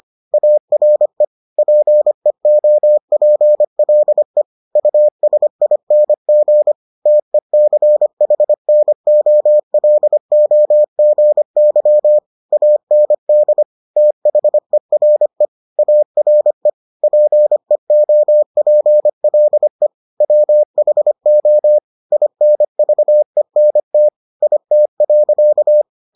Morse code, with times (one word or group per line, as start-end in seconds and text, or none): 0.3-1.2 ARE
1.6-4.4 PEOPLE
4.8-6.7 USING
7.1-12.2 TECHNOLOGY
12.5-13.6 AND
14.0-15.5 THERE
15.8-16.7 ARE
17.0-19.9 PEOPLE
20.2-21.8 WHO
22.1-24.1 INVENT
24.4-25.8 IT.